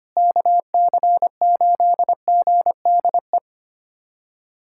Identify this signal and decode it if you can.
Morse code, heard as a keyed tone, about 25 wpm, 700 Hz. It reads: KC8GDE
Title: USA Callsign DE Callsign - 6 Characters - 25wpm